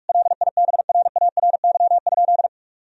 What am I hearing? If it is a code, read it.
Morse code, heard as a keyed tone, about 46 wpm, 700 Hz. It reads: LIBRARY?
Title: Sentences from Top 1000 Words - Review Part 4 - 45wpm